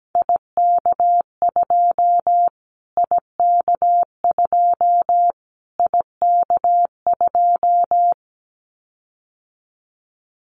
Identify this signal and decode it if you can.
Morse code, heard as a keyed tone, about 17 wpm, 700 Hz. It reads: IK2 IK2 IK2
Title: Callsign Prefixes for Top 25 DXCC - Repeated 3x - 17wpm